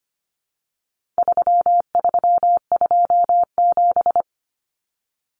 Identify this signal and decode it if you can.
Morse code, heard as a keyed tone, about 25 wpm, 700 Hz. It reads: 3327
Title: Common Cut Numbers - 4-Digit 25wpm